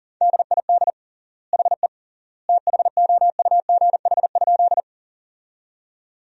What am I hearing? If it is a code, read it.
Morse code, heard as a keyed tone, about 37 wpm, 700 Hz. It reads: DID HE THOUGH?
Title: 3x Word-Spacing - Sentences from Top 400 Words - 40wpm